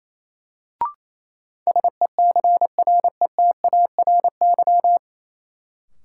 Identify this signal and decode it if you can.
Morse code, heard as a keyed tone, about 27 wpm, 700 Hz. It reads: SECRETARY
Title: Sentences from Top 1300 Words - New Word Drill Alt - 28wpm